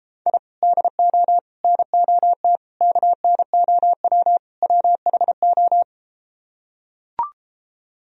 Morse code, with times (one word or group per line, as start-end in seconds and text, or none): 0.3-0.4 I
0.6-1.4 DO
1.6-2.6 NOT
2.8-4.4 KNOW
4.6-5.8 WHO